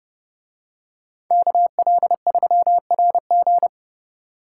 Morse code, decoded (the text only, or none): KL3RG